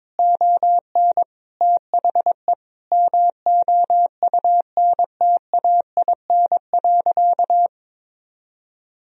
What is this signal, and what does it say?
Morse code, heard as a keyed tone, about 22 wpm, 700 Hz. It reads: ON THE MOUNTAIN.
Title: Sentences from Top 1000 Words - Review Part 1 - 22wpm